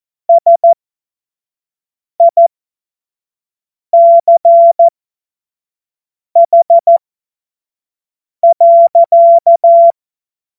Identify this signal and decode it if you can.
Morse code, heard as a keyed tone, about 14 wpm, spaced as for 5 wpm, 680 Hz. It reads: SICH.
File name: Morsekurs / Witze / Witz 04. Oktober 2022 25 F70